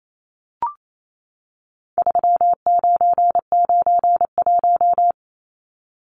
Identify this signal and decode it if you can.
Morse code, heard as a keyed tone, about 28 wpm, 700 Hz. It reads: 3991
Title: Numbers - 4-Digit 28wpm